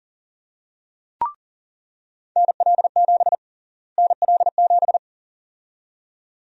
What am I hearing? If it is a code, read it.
Morse code, heard as a keyed tone, about 40 wpm, 700 Hz. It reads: NL7 NL7
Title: USA Call Sign Prefixes - Repeated 2x - 40wpm